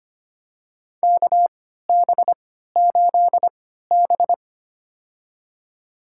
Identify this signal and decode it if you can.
Morse code, heard as a keyed tone, about 25 wpm, spaced as for 14 wpm, 700 Hz. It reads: KB8B
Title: USA Call Signs - 4 Characters - 3x Character Spacing - 25wpm